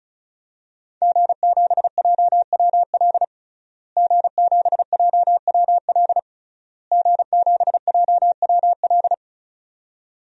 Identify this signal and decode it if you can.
Morse code, heard as a keyed tone, about 36 wpm, 700 Hz. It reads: G7JWL G7JWL G7JWL
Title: DXCC-100 Call Signs - 5 Characters - Repeated 3x - 35wpm